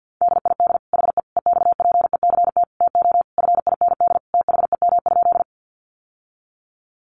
Morse code, heard as a keyed tone, about 50 wpm, 700 Hz. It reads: DID HE EXPECT TO FIND THEM?
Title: Sentences from Top 1100 Words 50wpm